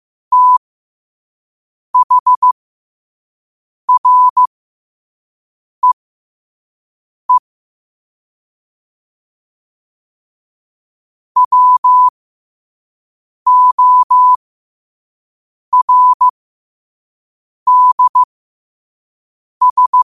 Morse code, heard as a keyed tone, about 15 wpm, spaced as for 5 wpm, 1000 Hz. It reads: THREE WORDS
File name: qotdfiles / 5wpm